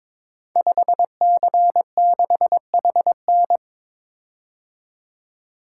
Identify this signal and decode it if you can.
Morse code, heard as a keyed tone, about 22 wpm, 700 Hz. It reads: 5C6HN
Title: Code Groups - 5 Letter:Numbers - 22wpm